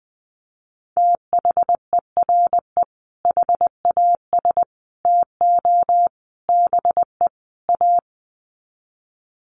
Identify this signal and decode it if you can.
Morse code, heard as a keyed tone, about 20 wpm, 700 Hz. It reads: THERE HAS TO BE A
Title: Sets of 5 Words Encore 20wpm